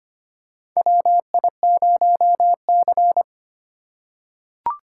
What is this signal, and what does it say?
Morse code, heard as a keyed tone, about 25 wpm, 700 Hz. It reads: WI0C
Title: USA Call Signs - 4 Characters - 4x Character Spacing - 25wpm